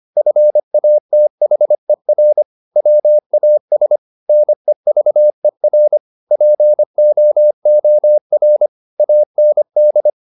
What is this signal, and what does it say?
Morse code, heard as a keyed tone, about 25 wpm, 600 Hz. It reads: FATHER WAS NEVER POOR AND
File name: the old man and the sea part 1 25wpm